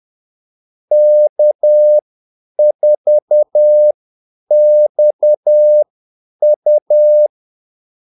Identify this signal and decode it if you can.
Morse code, heard as a keyed tone, about 10 wpm, spaced as for 8 wpm, 600 Hz. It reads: K4XU
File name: Callsigns-10-8wpm